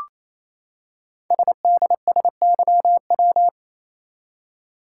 Morse code, heard as a keyed tone, about 27 wpm, 700 Hz. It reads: SDSYW
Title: Code Groups - 5 Letter:Numbers - 28wpm